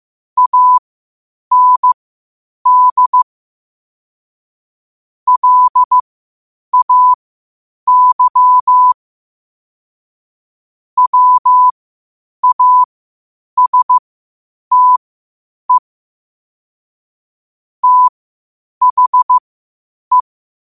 Morse code, heard as a keyed tone, about 15 wpm, spaced as for 8 wpm, 1000 Hz. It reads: AND LAY WASTE THE
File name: qotdfiles / 8wpm